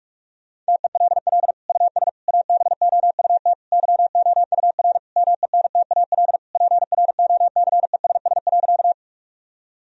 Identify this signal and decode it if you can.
Morse code, heard as a keyed tone, about 41 wpm, 700 Hz. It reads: TELL US ABOUT YOUR MENTAL PROCESS.
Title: Sentences from Top 1000 Words - Review Part 4 - 45wpm